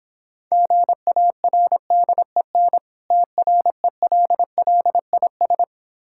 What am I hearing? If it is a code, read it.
Morse code, heard as a keyed tone, about 26 wpm, 700 Hz. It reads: GARDEN TRELLIS